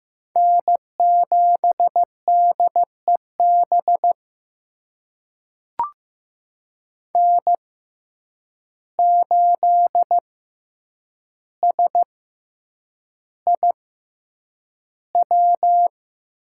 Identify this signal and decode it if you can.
Morse code, heard as a keyed tone, about 15 wpm, spaced as for 8 wpm, 700 Hz. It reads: N7DEB N8SIW
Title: USA Call Signs - 5 Characters - 6x Character Spacing - 15wpm